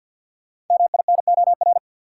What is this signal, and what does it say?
Morse code, heard as a keyed tone, about 50 wpm, 700 Hz. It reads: MINOR
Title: Sentences from Top 1500 Words - New Word Drill - 50wpm